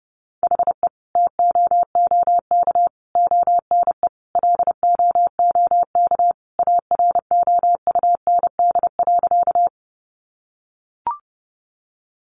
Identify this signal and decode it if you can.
Morse code, heard as a keyed tone, about 30 wpm, 700 Hz. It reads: HE TOOK ONE LOOK AROUND.
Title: Sentences from Top 600 Words 30wpm